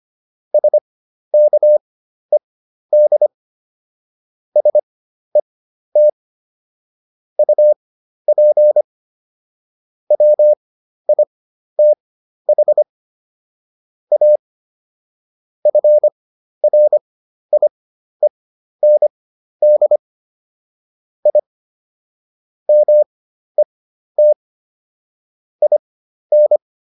Morse code, heard as a keyed tone, about 25 wpm, spaced as for 12 wpm, 600 Hz. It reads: SKED SET UP WITH A FRIEND I MET IN